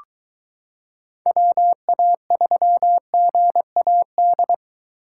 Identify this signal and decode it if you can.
Morse code, heard as a keyed tone, about 22 wpm, 700 Hz. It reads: WA3GAD